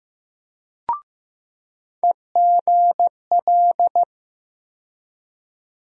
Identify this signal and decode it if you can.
Morse code, heard as a keyed tone, about 15 wpm, 700 Hz. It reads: EGL
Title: USA Call Sign Postfixes - 3 Letters 15wpm